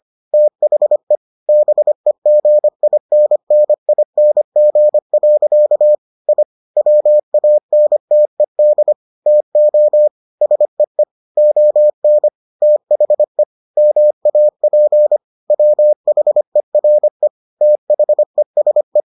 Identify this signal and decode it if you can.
Morse code, heard as a keyed tone, about 25 wpm, 600 Hz. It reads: THE BEGINNING. I WANTED TO SEE ON THE MAP WHERE THESE